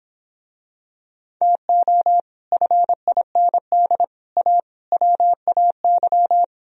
Morse code, heard as a keyed tone, about 26 wpm, 700 Hz. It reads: TO FIND A WAY